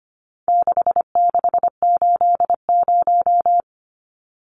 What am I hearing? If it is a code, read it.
Morse code, heard as a keyed tone, about 25 wpm, 700 Hz. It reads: 6680
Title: Numbers - 4-Digit 25wpm